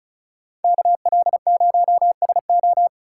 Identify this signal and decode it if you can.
Morse code, heard as a keyed tone, about 43 wpm, 700 Hz. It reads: KL0SO
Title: Speed Racing - USA 2x2 Call Signs 35wpm